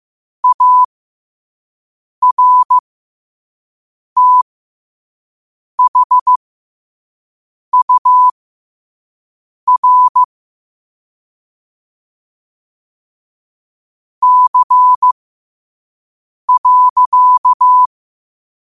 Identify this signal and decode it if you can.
Morse code, heard as a keyed tone, about 15 wpm, spaced as for 5 wpm, 1000 Hz. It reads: ARTHUR C.
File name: qotdfiles / 5wpm